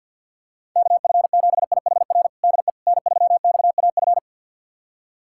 Morse code, heard as a keyed tone, about 51 wpm, 700 Hz. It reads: KF7IHR DE N3XAF
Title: USA Callsign DE Callsign 50wpm